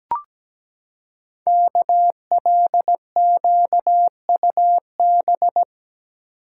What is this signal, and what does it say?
Morse code, heard as a keyed tone, about 17 wpm, 700 Hz. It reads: KLQUB
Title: Code Groups - 5 Letter:Numbers - 17wpm